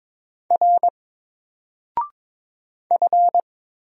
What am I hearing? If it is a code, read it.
Morse code, heard as a keyed tone, about 22 wpm, 700 Hz. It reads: R F
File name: Single-Letter-Number-22wpm